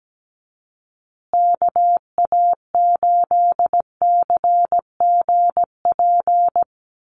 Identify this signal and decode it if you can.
Morse code, heard as a keyed tone, about 17 wpm, 700 Hz. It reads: KA8CGP
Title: USA Call Signs Repeated 2x 17wpm